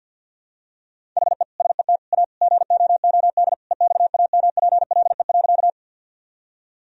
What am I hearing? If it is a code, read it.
Morse code, heard as a keyed tone, about 50 wpm, 700 Hz. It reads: HE SET A GOOD EXAMPLE.
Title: Sentences from Top 1000 Words - Review Part 1 - 50wpm